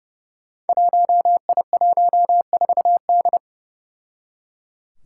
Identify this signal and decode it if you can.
Morse code, heard as a keyed tone, about 30 wpm, 700 Hz. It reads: 1I14D
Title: Five Challenging Letter-Number Combinations 30wpm